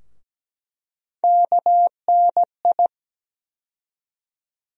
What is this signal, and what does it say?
Morse code, heard as a keyed tone, about 17 wpm, 700 Hz. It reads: KNI
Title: USA Call Sign Postfixes - 3 Letters 17wpm